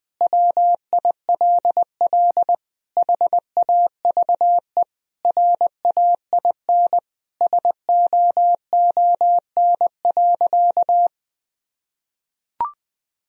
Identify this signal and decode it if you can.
Morse code, heard as a keyed tone, about 20 wpm, 700 Hz. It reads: WILL HAVE RAIN SOON.
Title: Sentences from Top 1000 Words - Review Part 1 - 20wpm